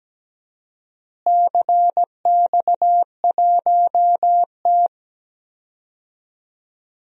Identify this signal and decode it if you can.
Morse code, heard as a keyed tone, about 17 wpm, 700 Hz. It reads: CX1T
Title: DXCC-100 Call Signs - 4 Characters - Repeated 2x - 17wpm